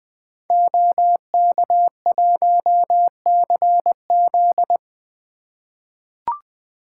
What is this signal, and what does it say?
Morse code, heard as a keyed tone, about 20 wpm, 700 Hz. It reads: OK1CZ
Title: 2023 SOTA Activator Call Signs - 20wpm